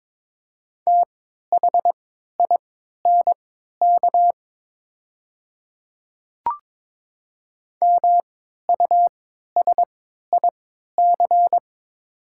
Morse code, heard as a keyed tone, about 23 wpm, spaced as for 13 wpm, 700 Hz. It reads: THINK MUSIC
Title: 3x Character Spacing - Top 300 Words - 22wpm